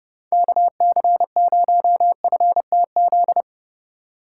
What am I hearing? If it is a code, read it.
Morse code, heard as a keyed tone, about 31 wpm, 700 Hz. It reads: KC0FTZ
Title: USA Call Signs - 6 Characters - 2x Character Spacing - 30wpm